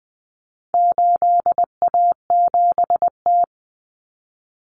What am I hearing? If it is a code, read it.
Morse code, heard as a keyed tone, about 20 wpm, 700 Hz. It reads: 8A7T
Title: Common Cut Numbers - 4-Digit 20wpm